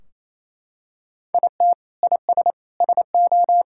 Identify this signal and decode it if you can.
Morse code, heard as a keyed tone, about 28 wpm, 700 Hz. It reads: IT IS SO
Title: Sentences from Top 1600 Words 28wpm